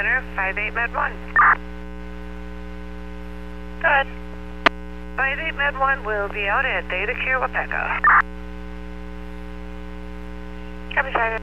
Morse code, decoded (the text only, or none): none